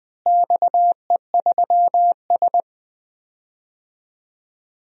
Morse code, heard as a keyed tone, about 21 wpm, 700 Hz. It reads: XE3S
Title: DXCC-100 Call Signs - 4 Characters - Repeated 3x - 20wpm